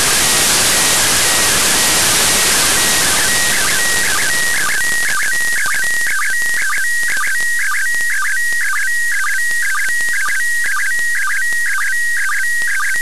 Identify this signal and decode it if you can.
SSTV (slow-scan television), PD 120.
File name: practicas-en-abierto / decodifica-imagenes-iss